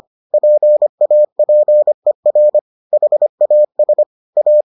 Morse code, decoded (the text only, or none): PAPER HAS A